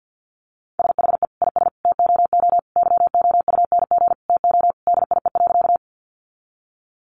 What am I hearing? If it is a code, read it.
Morse code, heard as a keyed tone, about 50 wpm, 700 Hz. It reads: SHE IS TOO YOUNG TO DIE.
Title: Sentences from Top 1300 Words 50wpm